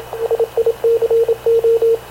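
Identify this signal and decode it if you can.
Morse code, heard as a keyed tone, about 26 wpm, 460 Hz, with radio noise. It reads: SICO
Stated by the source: Havana Radio / CLA sending Spanish press (PX) in Morse, recorded by Radio Officer Paul Dumesnil onboard CNAV QUEST/CZDO.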